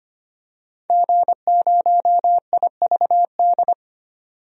Morse code, heard as a keyed tone, about 25 wpm, 700 Hz. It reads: G0IVD
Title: DXCC-100 Call Signs - 5 Characters - 25wpm